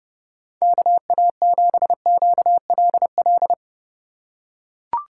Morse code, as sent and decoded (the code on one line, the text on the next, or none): -.- .- --... --.- .-.. .-..
KA7QLL